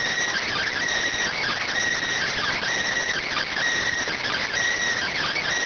Emergency Alert System (SAME) tones (header bursts), heard as harsh buzzing bursts. It is heard from 0.3 to 4.5 s.